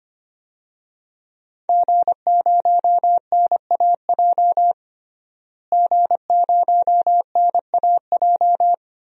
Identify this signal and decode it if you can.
Morse code, heard as a keyed tone, about 25 wpm, 700 Hz. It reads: G0NAJ G0NAJ
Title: DXCC-100 Call Signs - 5 Characters - Repeated 2x - 25wpm